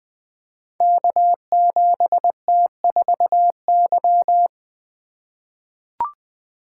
Morse code, heard as a keyed tone, about 20 wpm, 700 Hz. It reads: K7T4Y
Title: Code Groups - 5 Letter:Numbers - 20wpm